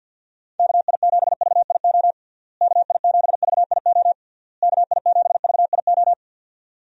Morse code, heard as a keyed tone, about 52 wpm, 700 Hz. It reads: KI7VIO KI7VIO KI7VIO
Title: USA Call Signs - 6 Characters - Repeated 3x - 50wpm